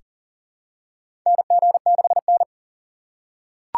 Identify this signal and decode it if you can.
Morse code, heard as a keyed tone, about 40 wpm, 700 Hz. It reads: NG6N
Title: USA Call Signs - 4 Characters - 3x Character Spacing - 40wpm